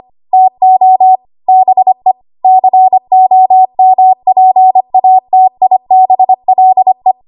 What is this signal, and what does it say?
Morse code, heard as a keyed tone, about 25 wpm, 750 Hz. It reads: TO BE COMPATIBLE
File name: Archive / 25 WPM / 25WPM